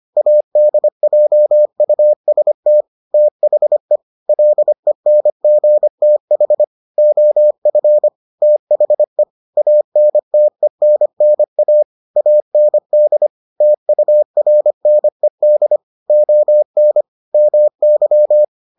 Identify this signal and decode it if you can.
Morse code, heard as a keyed tone, about 25 wpm, 600 Hz. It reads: ADJUST THE LENGTH OF THE ANTENNA AND TURNED ON MY